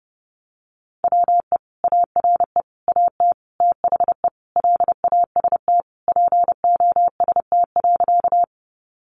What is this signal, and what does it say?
Morse code, heard as a keyed tone, about 30 wpm, 700 Hz. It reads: WE ARE AT THE LAST POST.
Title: Sentences from Top 1000 Words 30wpm